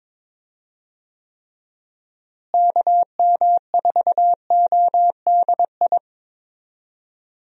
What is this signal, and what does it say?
Morse code, heard as a keyed tone, about 22 wpm, 700 Hz. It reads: KM4ODI
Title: USA Call Signs - 6 Characters - Repeated 2x - 22wpm